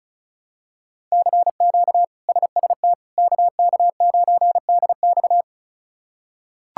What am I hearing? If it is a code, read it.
Morse code, heard as a keyed tone, about 36 wpm, 700 Hz. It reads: CQ SST KK9DX
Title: K1USN SST QSO Practice - 35wpm